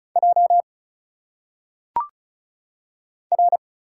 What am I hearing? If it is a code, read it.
Morse code, heard as a keyed tone, about 36 wpm, 700 Hz. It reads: J R
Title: Single Letters - Phonetic Pronunciation 35wpm